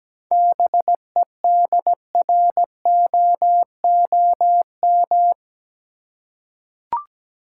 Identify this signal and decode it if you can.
Morse code, heard as a keyed tone, about 17 wpm, 700 Hz. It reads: BEDROOM
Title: Seven Letter Words - 17wpm